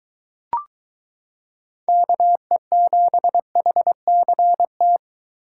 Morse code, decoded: KE7HCT

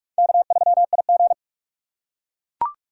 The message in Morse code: K3IG

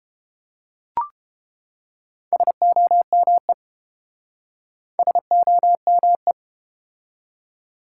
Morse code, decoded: SOME SOME

